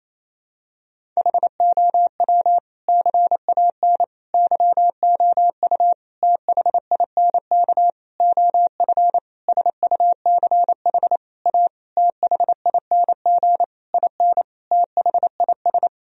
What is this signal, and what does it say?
Morse code, heard as a keyed tone, about 28 wpm, 700 Hz. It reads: HOW CAN YOU THINK OF SUCH A THING IN THIS
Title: Sentences from Top 1000 Words - Review Part 3 - 28wpm